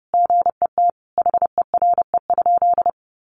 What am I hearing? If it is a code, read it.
Morse code, heard as a keyed tone, about 30 wpm, 700 Hz. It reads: GET HERE?
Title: Sentences from Top 1400 Words 30wpm